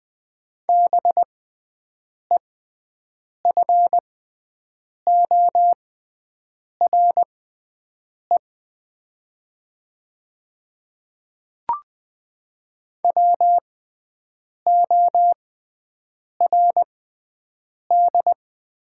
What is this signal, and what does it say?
Morse code, heard as a keyed tone, about 20 wpm, spaced as for 7 wpm, 700 Hz. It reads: BEFORE WORD
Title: 6x Character Spacing - Top 200 Words - 20wpm